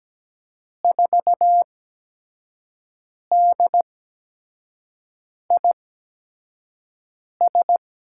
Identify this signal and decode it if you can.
Morse code, heard as a keyed tone, about 17 wpm, spaced as for 5 wpm, 700 Hz. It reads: 4DIS
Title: USA Call Signs - 8x Character Spacing - 17wpm